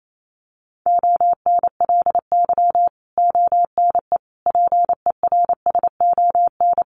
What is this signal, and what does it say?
Morse code, heard as a keyed tone, about 28 wpm, 700 Hz. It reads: ONLY ONE PERSON
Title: Sentences from Top 1100 Words 28wpm